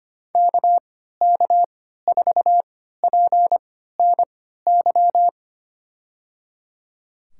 Morse code, heard as a keyed tone, about 25 wpm, spaced as for 14 wpm, 690 Hz. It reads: KK4PNY
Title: USA Call Signs - 6 Characters - 3x Character Spacing - 25wpm